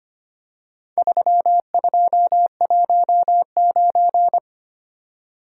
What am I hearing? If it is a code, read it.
Morse code, heard as a keyed tone, about 25 wpm, 700 Hz. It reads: 3219
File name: practice / 25wpm / Four Letter-Number Combinations - 25wpm